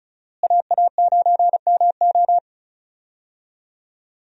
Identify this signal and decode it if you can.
Morse code, heard as a keyed tone, about 34 wpm, 700 Hz. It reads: AA9MO
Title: USA Call Signs - 5 Characters - Repeated 2x - 35wpm